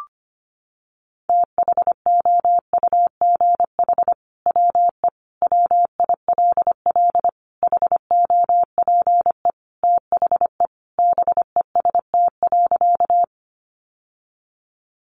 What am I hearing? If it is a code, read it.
Morse code, heard as a keyed tone, about 25 wpm, 700 Hz. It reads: THOUGH WE WILL HOPE THE BEST.
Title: Sentences from Top 1000 Words 25wpm